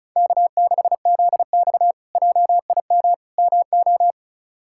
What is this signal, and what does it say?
Morse code, heard as a keyed tone, about 36 wpm, 700 Hz. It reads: K6ZX JIM MO
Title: K1USN SST QSO Practice - 35wpm